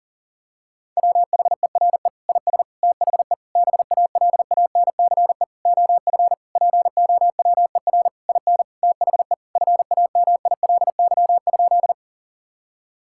Morse code, heard as a keyed tone, about 40 wpm, 700 Hz. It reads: WHERE IS THE BALANCE OF POWER IN THE FAMILY?
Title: Sentences from Top 1000 Words - Review Part 6 - 40wpm